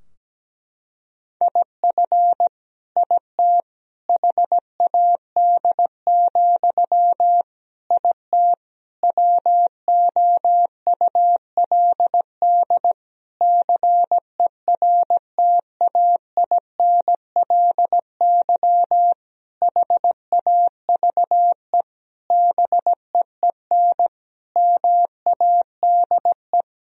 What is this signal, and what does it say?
Morse code, heard as a keyed tone, about 17 wpm, 700 Hz. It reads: IF IT HAD, IT WOULD CERTAINLY HAVE BEEN MADE
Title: Sentences from Top 1000 Words - Review Part 3 - 17wpm